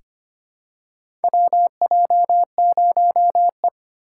Morse code, heard as a keyed tone, about 25 wpm, 700 Hz. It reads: WJ0E
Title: USA Call Signs - 4 Characters - 4x Character Spacing - 25wpm